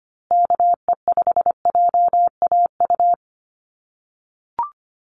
Morse code, heard as a keyed tone, about 25 wpm, 700 Hz. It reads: KE5JAU
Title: USA Call Signs Repeated 2x 25wpm